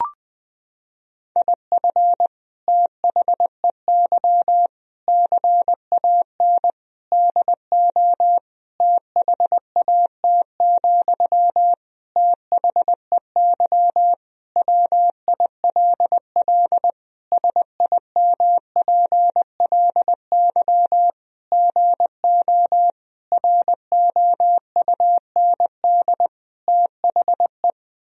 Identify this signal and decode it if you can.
Morse code, heard as a keyed tone, about 20 wpm, 700 Hz. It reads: IF THEY CAN DO THAT, THEY WILL SIMPLY GO ROUND THE
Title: Sentences from Top 1000 Words - Review Part 4 - 20wpm